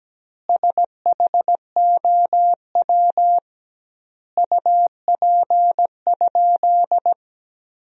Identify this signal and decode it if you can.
Morse code, heard as a keyed tone, about 17 wpm, 700 Hz. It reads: SHOW UP?
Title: 2x Word-Spacing - Sentences from Top 200 Words - 17wpm